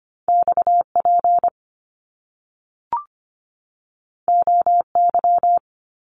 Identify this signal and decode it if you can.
Morse code, heard as a keyed tone, about 25 wpm, 700 Hz. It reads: X P O Y
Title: Top Two Letter Combinations in Words 25wpm